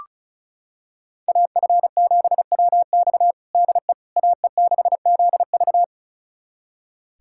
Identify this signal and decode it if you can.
Morse code, heard as a keyed tone, about 36 wpm, 700 Hz. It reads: AF7WX DE AE6ZV
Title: USA Callsign DE Callsign - 5 Characters - 35wpm